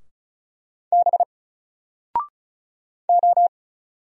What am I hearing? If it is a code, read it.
Morse code, heard as a keyed tone, about 35 wpm, 700 Hz. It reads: B O
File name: Single-Letter-Number-35wpm